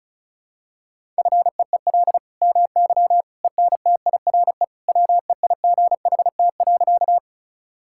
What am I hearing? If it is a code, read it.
Morse code, heard as a keyed tone, about 36 wpm, 700 Hz. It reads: FEEL MY ENTIRE WEIGHT.
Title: Sentences from Top 1000 Words - Review Part 5 - 35wpm